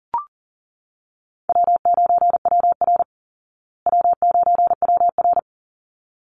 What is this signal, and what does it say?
Morse code, heard as a keyed tone, about 40 wpm, 700 Hz. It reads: W9WR W9WR
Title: Easy Call Signs Repeated 2x - 40wpm